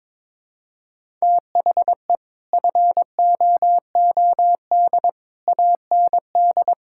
Morse code, heard as a keyed tone, about 22 wpm, 700 Hz. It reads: THE FOOD AND
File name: practice / 22wpm / Build A Phrase - 5-Word Phrases - 22wpm